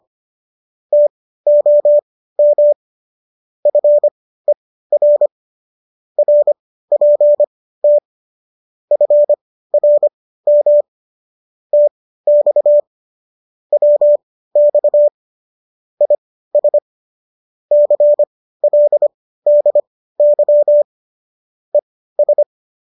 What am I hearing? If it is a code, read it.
Morse code, heard as a keyed tone, about 25 wpm, spaced as for 15 wpm, 590 Hz. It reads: TOM FER RPT FRM TX WX IS CLDY ES